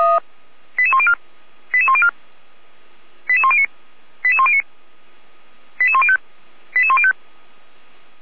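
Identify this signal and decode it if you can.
Five-tone selective call (ZVEI), heard as quick runs of five tones, tones of about 70 ms. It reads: 88104 88104 88109 88109 88105 88105
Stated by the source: ZVEI five-tone selective call 88383, received on an SDR.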